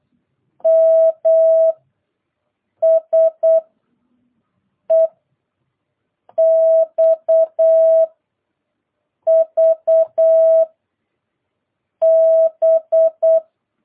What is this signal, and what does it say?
Morse code, heard as a keyed tone, about 8 wpm, 660 Hz. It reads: M S E X V B